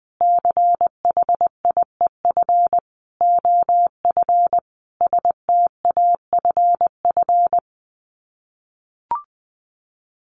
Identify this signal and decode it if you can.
Morse code, heard as a keyed tone, about 20 wpm, 700 Hz. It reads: CHIEF OF STAFF
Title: Sets of 5 Words Encore 20wpm